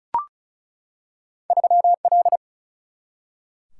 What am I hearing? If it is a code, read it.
Morse code, heard as a keyed tone, about 36 wpm, 700 Hz. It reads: 3L
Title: Single-Letter Single-Number 35wpm